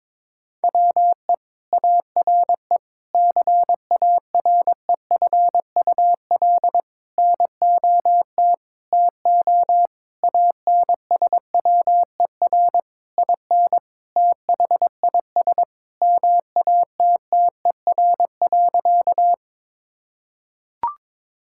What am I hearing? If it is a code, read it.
Morse code, heard as a keyed tone, about 22 wpm, 700 Hz. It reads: WE ARE CAREFUL NOT TO ANSWER IN THIS MATTER.
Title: Sentences from Top 1600 Words 22wpm